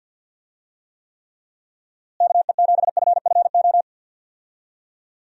Morse code, heard as a keyed tone, about 52 wpm, 700 Hz. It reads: KE7FFO
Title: USA Call Signs - 6 Characters - Repeated 3x - 50wpm